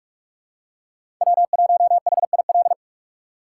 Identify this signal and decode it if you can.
Morse code, heard as a keyed tone, about 43 wpm, 700 Hz. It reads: W1HIL
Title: USA Call Signs - 5 Characters - 2x Character Spacing - 45wpm